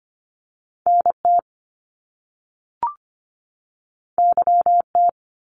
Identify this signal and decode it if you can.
Morse code, heard as a keyed tone, about 25 wpm, 700 Hz. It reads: NT YT
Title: Canadian Provinces and Territories 25wpm